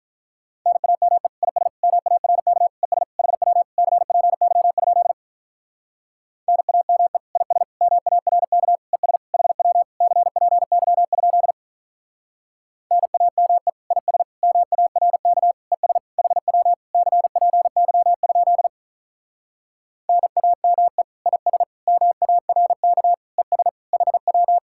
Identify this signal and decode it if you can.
Morse code, heard as a keyed tone, about 43 wpm, 700 Hz. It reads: NAME IS MARK ES HW CPY? NAME IS MARK ES HW CPY? NAME IS MARK ES HW CPY? NAME IS MARK ES HW
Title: Speed-Racing Common QSO Phrases - 35wpm